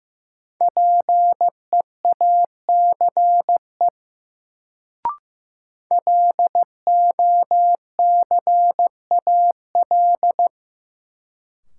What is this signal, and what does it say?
Morse code, heard as a keyed tone, about 15 wpm, 700 Hz. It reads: PEACE LOCAL
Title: Sentences from Top 1500 Words - New Word Drill Alt - 15wpm